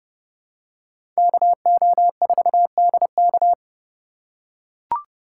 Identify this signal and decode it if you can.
Morse code, heard as a keyed tone, about 29 wpm, 700 Hz. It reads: KO4DK
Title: USA Call Signs - 5 Characters - 2x Character Spacing - 30wpm